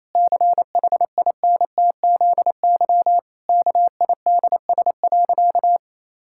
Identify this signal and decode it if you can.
Morse code, heard as a keyed tone, about 28 wpm, 700 Hz. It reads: CHINTZY KIDS.
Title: Pangram Challenge v2 - 28wpm